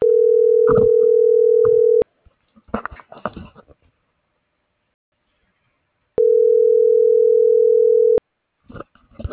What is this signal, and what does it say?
Ringback tone (North America), 440 + 480 Hz.